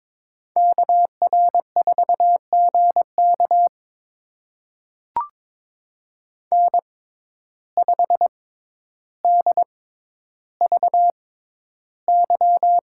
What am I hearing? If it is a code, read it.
Morse code, heard as a keyed tone, about 22 wpm, spaced as for 11 wpm, 700 Hz. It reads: KR4GK N5DVY
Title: USA Call Signs - 5 Characters - 6x Character Spacing - 22wpm